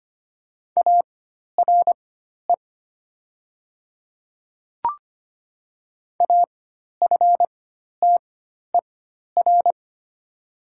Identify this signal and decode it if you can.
Morse code, heard as a keyed tone, about 25 wpm, spaced as for 12 wpm, 700 Hz. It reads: ARE AFTER
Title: 4x Character Spacing - Top 200 Words - 25wpm